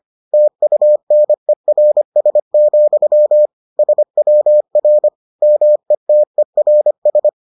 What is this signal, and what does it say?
Morse code, heard as a keyed tone, about 25 wpm, 600 Hz. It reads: TUNERS, SWR METERS